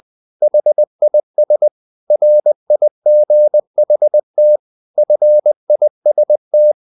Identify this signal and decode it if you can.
Morse code, heard as a keyed tone, about 20 wpm, 600 Hz. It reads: HIS RIGHT FIST
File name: the old man and the sea part 2 20wpm